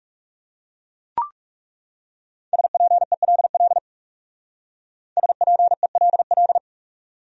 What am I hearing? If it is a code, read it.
Morse code, heard as a keyed tone, about 38 wpm, 700 Hz. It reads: SPELL SPELL